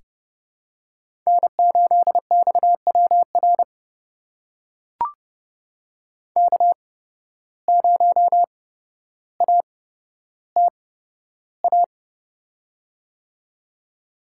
Morse code, heard as a keyed tone, about 30 wpm, spaced as for 13 wpm, 700 Hz. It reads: N8XWR K0ATA
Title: USA Call Signs - 5 Characters - 8x Character Spacing - 30wpm